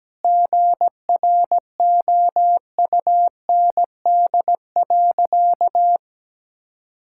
Morse code, heard as a keyed tone, about 17 wpm, 700 Hz. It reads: GROUND.
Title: Sentences from Top 1000 Words - Review Part 5 - 17wpm